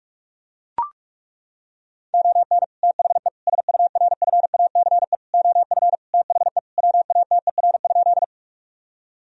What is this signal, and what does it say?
Morse code, heard as a keyed tone, about 41 wpm, 700 Hz. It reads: ON THE SURFACE OF THE WATER?
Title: Sentences from Top 1000 Words - Review Part 3 - 45wpm